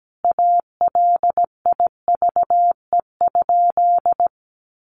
Morse code, heard as a keyed tone, about 17 wpm, 700 Hz. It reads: ALIVE?
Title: A Sound Of Thunder 17wpm